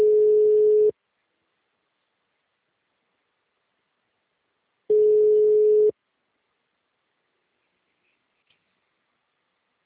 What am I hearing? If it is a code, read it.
Ringback tone (Europe, 425 Hz).